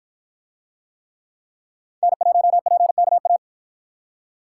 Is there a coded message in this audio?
Yes: Morse code from 2.0 to 3.4 s, heard as a keyed tone, about 53 wpm, 700 Hz. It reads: N1PKA